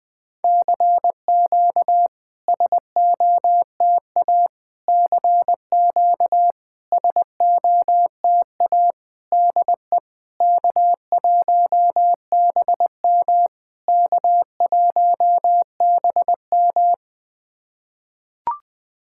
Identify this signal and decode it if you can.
Morse code, heard as a keyed tone, about 20 wpm, 690 Hz. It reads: CQ SOTA CQ SOTA DE K1BM K1BM
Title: SOTA QSO Practice 20wpm